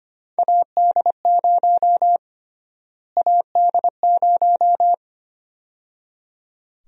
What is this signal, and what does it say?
Morse code, heard as a keyed tone, about 25 wpm, 700 Hz. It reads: AD0 AD0
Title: USA Call Sign Prefixes - Repeated 2x - 25wpm